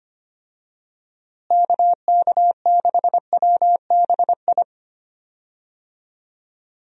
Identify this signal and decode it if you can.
Morse code, heard as a keyed tone, about 25 wpm, 700 Hz. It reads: KK6WBI
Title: USA Call Signs - 6 Characters - Repeated 3x - 25wpm